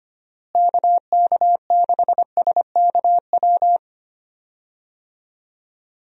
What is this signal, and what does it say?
Morse code, heard as a keyed tone, about 25 wpm, 700 Hz. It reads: KK6SKW
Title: USA Call Signs - 6 Characters - Repeated 3x - 25wpm